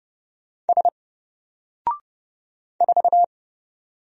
Morse code, heard as a keyed tone, about 30 wpm, 700 Hz. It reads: S 4
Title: Single Character (Letter:Number:Punctuation:Prosign) Alt 30wpm